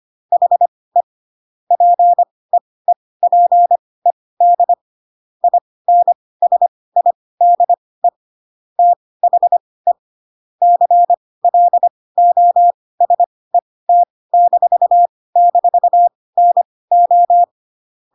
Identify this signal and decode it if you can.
Morse code, heard as a keyed tone, about 25 wpm, spaced as for 18 wpm, 700 Hz. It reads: HE PEEPED INSIDE THE CLOSET--NO
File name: Pinocchio / Pinocchio ch1 12 25 ik1hge